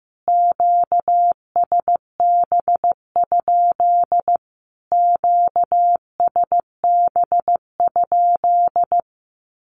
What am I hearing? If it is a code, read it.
Morse code, heard as a keyed tone, about 15 wpm, 700 Hz. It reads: QSB? QSB?
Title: Q-Codes Repeated 3x 15wpm